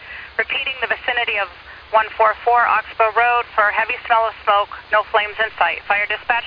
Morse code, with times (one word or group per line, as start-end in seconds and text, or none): none